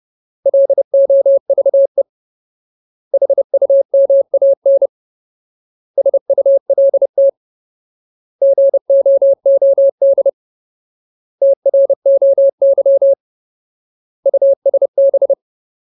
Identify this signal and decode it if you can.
Morse code, heard as a keyed tone, about 30 wpm, 550 Hz. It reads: LOVE HUMAN SULT GOOD TROY USB
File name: Ultimate Word List / 30WPM Ultimate